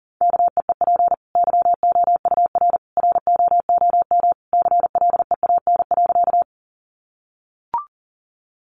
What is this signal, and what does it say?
Morse code, heard as a keyed tone, about 40 wpm, 700 Hz. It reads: KEEP YOUR ROOM CLEAN.
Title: Sentences from Top 1100 Words 40wpm